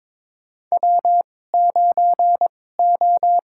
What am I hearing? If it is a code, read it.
Morse code, heard as a keyed tone, about 22 wpm, spaced as for 16 wpm, 700 Hz. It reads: W9O
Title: USA Call Signs - 2x Character Spacing - 22wpm